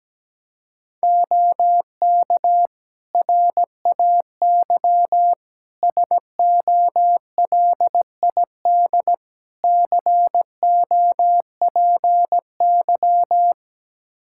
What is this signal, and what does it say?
Morse code, heard as a keyed tone, about 17 wpm, 700 Hz. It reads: OK RAY SOLID COPY
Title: Practice QSO - Exchange 3 - More Info - 17wpm